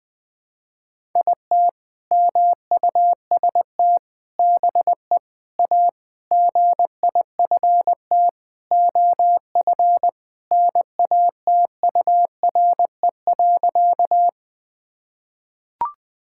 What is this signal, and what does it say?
Morse code, heard as a keyed tone, about 20 wpm, 700 Hz. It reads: IT MUST BE A GIFT OF NATURE.
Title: Sentences from Top 1000 Words - Review Part 6 - 20wpm